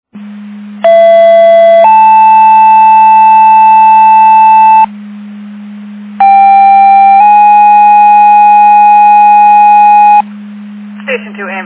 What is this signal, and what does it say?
Two-tone page, heard as a one-second tone, then a three-second tone, about 670 then 880 Hz, about 790 then 830 Hz, with speech. It is heard from 0.8 to 10.2 s.